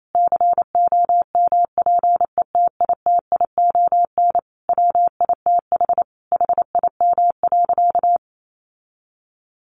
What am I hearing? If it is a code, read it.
Morse code, heard as a keyed tone, about 28 wpm, 700 Hz. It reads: COMPETITION WITH HIM.
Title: Sentences from Top 1000 Words 28wpm